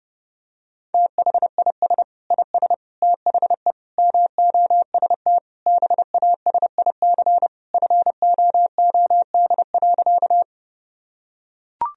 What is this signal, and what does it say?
Morse code, heard as a keyed tone, about 30 wpm, 700 Hz. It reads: THIS IS THE MOST BASIC FOOD.
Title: Sentences from Top 1000 Words - Review Part 3 - 30wpm